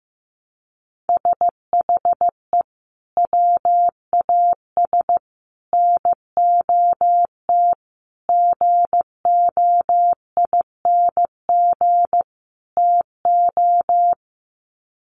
Morse code, heard as a keyed tone, about 15 wpm, 700 Hz. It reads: SHE WAS NOT GOING TO
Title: Sets of 5 Words Encore2 15wpm